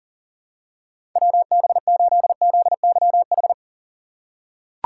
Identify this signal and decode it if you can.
Morse code, heard as a keyed tone, about 40 wpm, 700 Hz. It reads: WB8ZYH
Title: USA Call Signs - 6 Characters - 3x Character Spacing - 40wpm